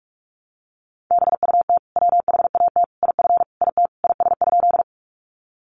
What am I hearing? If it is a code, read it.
Morse code, heard as a keyed tone, about 44 wpm, 700 Hz. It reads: BUT WHAT IF IT IS?
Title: Sentences from Top 100 Words 45wpm